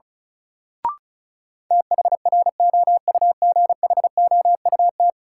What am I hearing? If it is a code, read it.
Morse code, heard as a keyed tone, about 36 wpm, 700 Hz. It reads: THROUGHOUT
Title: Sentences from Top 800 Words - New Word Drill - 35wpm